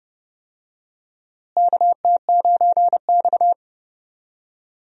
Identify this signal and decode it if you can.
Morse code, heard as a keyed tone, about 29 wpm, 700 Hz. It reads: KT9X
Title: USA Call Signs - 4 Characters - Repeated 2x - 30wpm